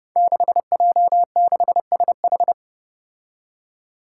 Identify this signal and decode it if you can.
Morse code, heard as a keyed tone, about 32 wpm, 700 Hz. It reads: 6J6SH
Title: Five Challenging Letter-Number Combinations 30wpm